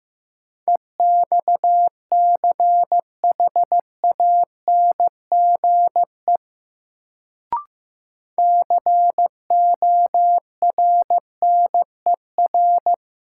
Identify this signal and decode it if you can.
Morse code, heard as a keyed tone, about 16 wpm, 700 Hz. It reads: EXCHANGE CORNER
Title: Sets of 5 Words - New Word Drill - 15wpm